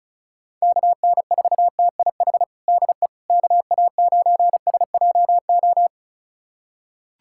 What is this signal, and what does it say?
Morse code, heard as a keyed tone, about 34 wpm, 700 Hz. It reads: KN4TIH DE KA9SJO
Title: USA Callsign DE Callsign - 6 Characters - 35wpm